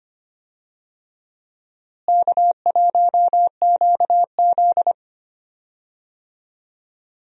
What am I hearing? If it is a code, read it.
Morse code, heard as a keyed tone, about 25 wpm, 700 Hz. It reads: K1QZ
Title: USA Call Signs - 4 Characters - Repeated 3x - 25wpm